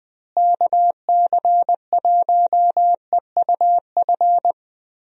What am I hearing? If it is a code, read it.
Morse code, heard as a keyed tone, about 21 wpm, 700 Hz. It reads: KC1EUF
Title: USA Call Signs - 6 Characters - Repeated 2x - 20wpm